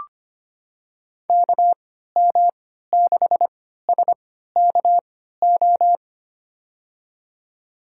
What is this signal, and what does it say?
Morse code, heard as a keyed tone, about 25 wpm, spaced as for 14 wpm, 700 Hz. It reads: KM6SKO
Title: USA Call Signs - 6 Characters - 3x Character Spacing - 25wpm